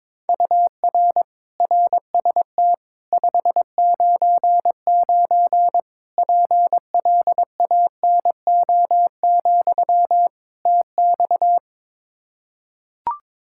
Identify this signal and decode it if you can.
Morse code, heard as a keyed tone, about 22 wpm, 700 Hz. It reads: UR RST 599 PLANO, TX
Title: K1USN SST QSO Practice 22wpm